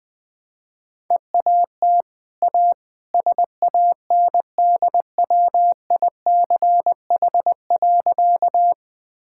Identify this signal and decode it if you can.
Morse code, heard as a keyed tone, about 20 wpm, 700 Hz. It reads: EAT A SANDWICH.
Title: Easy Sentences with Hints - 20wpm